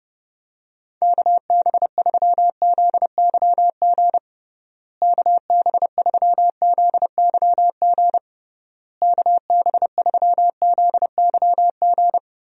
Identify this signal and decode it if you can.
Morse code, heard as a keyed tone, about 30 wpm, 700 Hz. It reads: KB3ZYG KB3ZYG KB3ZYG
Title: USA Call Signs - 6 Characters - Repeated 3x - 30wpm